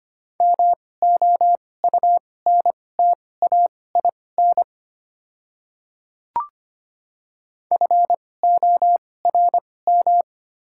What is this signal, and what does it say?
Morse code, heard as a keyed tone, about 25 wpm, spaced as for 18 wpm, 690 Hz. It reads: MOUNTAIN FORM
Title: 2x Character Spacing - Top 300 Words - 25wpm